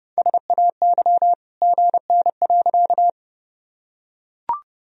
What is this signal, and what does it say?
Morse code, heard as a keyed tone, about 30 wpm, 700 Hz. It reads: SAY GN.
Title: Sentences from Top 300 Words and Abbreviations 30wpm